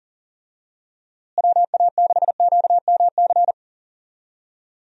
Morse code, heard as a keyed tone, about 37 wpm, 700 Hz. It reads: WA6QMC
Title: USA Call Signs - 6 Characters - Repeated 2x - 40wpm